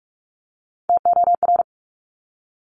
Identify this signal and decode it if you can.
Morse code, heard as a keyed tone, about 46 wpm, 700 Hz. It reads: TOR
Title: Top Three Letter Combinations in Words 45wpm